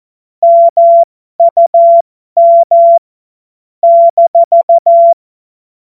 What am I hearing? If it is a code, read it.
Morse code, heard as a keyed tone, about 14 wpm, spaced as for 12 wpm, 680 Hz. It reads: MUM -
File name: Morsekurs / Witze